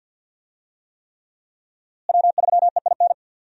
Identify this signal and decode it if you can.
Morse code, heard as a keyed tone, about 50 wpm, 700 Hz. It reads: W3EIN
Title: USA Call Signs - 5 Characters - Repeated 3x - 50wpm